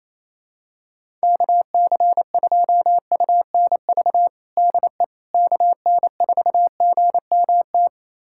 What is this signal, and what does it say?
Morse code, heard as a keyed tone, about 27 wpm, 700 Hz. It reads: KC2UNV DE KN4GMT
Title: USA Callsign DE Callsign - 6 Characters - 28wpm